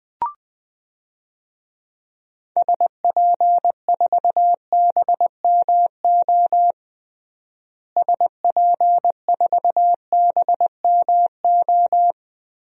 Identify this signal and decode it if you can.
Morse code, heard as a keyed tone, about 20 wpm, 700 Hz. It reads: SP4BMO SP4BMO
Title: DXCC-100 Call Signs - 6 Characters - Repeated 3x - 20wpm